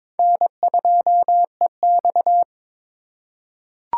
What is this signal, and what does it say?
Morse code, heard as a keyed tone, about 22 wpm, 700 Hz. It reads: N2EX
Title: USA Call Signs - 4 Characters - 4x Character Spacing - 22wpm